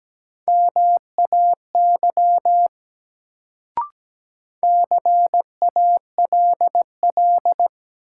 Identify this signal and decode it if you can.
Morse code, heard as a keyed tone, about 17 wpm, 700 Hz. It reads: MAY CALL
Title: Sentences from Top 100 Words - New Word Drill - 17wpm